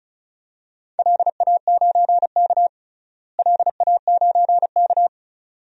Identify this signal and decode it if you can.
Morse code, heard as a keyed tone, about 36 wpm, 700 Hz. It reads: LA9K LA9K
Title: DXCC-100 Call Signs - 4 Characters - Repeated 2x - 35wpm